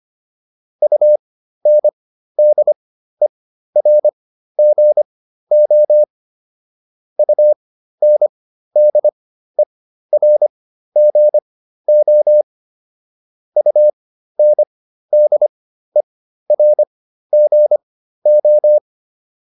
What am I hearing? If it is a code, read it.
Morse code, heard as a keyed tone, about 25 wpm, spaced as for 13 wpm, 600 Hz. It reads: UNDERGO UNDERGO UNDERGO